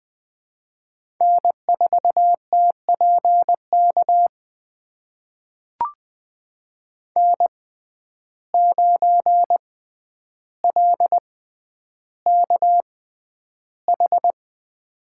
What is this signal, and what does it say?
Morse code, heard as a keyed tone, about 20 wpm, spaced as for 10 wpm, 700 Hz. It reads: N4TPK N9LKH
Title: USA Call Signs - 5 Characters - 6x Character Spacing - 20wpm